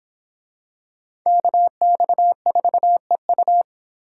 Morse code, heard as a keyed tone, about 26 wpm, 700 Hz. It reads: KX4EU